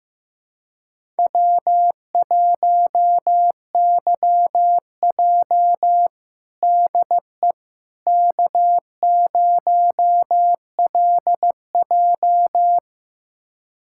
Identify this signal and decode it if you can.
Morse code, heard as a keyed tone, about 15 wpm, 700 Hz. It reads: W1YJ DE K0LJ
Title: USA Callsign DE Callsign - 4 Characters - 15wpm